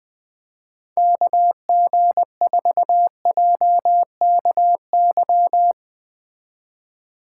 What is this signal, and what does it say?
Morse code, heard as a keyed tone, about 20 wpm, 700 Hz. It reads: KG4JKY